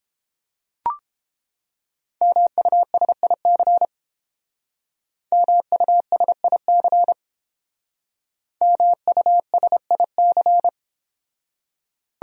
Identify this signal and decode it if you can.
Morse code, heard as a keyed tone, about 30 wpm, 700 Hz. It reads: MUSIC MUSIC MUSIC